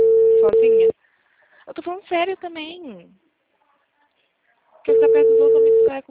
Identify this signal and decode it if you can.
Ringback tone (China, 450 Hz), with speech.
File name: out- -2011- - - .628